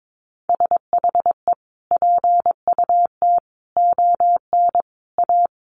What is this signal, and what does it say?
Morse code, heard as a keyed tone, about 22 wpm, 700 Hz. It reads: SHE PUT ON A
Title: Sentences from Top 1200 Words 22wpm